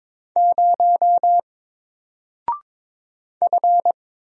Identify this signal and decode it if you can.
Morse code, heard as a keyed tone, about 22 wpm, 700 Hz. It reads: 0 F
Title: Single Character (Letter:Number:Punctuation:Prosign) Alt 22wpm